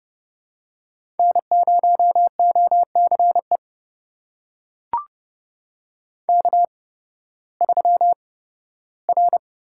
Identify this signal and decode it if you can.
Morse code, heard as a keyed tone, about 30 wpm, spaced as for 16 wpm, 700 Hz. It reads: N0OCE K3R
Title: USA Call Signs - 5 Characters - 8x Character Spacing - 30wpm